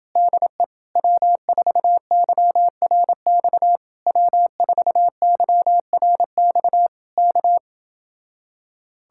Morse code, heard as a keyed tone, about 29 wpm, 700 Hz. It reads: DE W4YRX W4YRX K